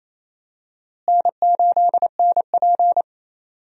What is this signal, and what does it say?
Morse code, heard as a keyed tone, about 28 wpm, 700 Hz. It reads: N8NP
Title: USA Call Signs - 4 Characters - Repeated 2x - 28wpm